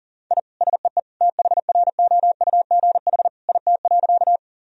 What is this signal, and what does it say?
Morse code, heard as a keyed tone, about 40 wpm, 700 Hz. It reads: I SEE THROUGH IT.
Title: Sentences from Top 1000 Words - Review Part 1 - 40wpm